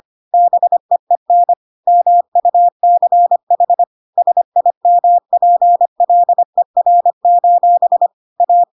Morse code, heard as a keyed tone, about 25 wpm, 700 Hz. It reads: BEEN MUCH SIMPLER: A
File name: Pinocchio / Pinocchio ch3 25 25 ik1hge